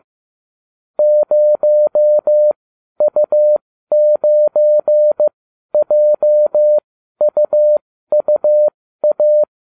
Morse code, heard as a keyed tone, about 15 wpm, spaced as for 12 wpm, 600 Hz. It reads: 0U9JUUA